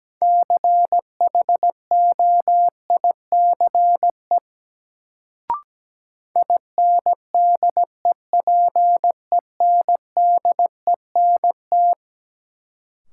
Morse code, heard as a keyed tone, about 17 wpm, 700 Hz. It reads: CHOICE INDEPENDENT
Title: Sentences from Top 900 Words - New Word Drill - 17wpm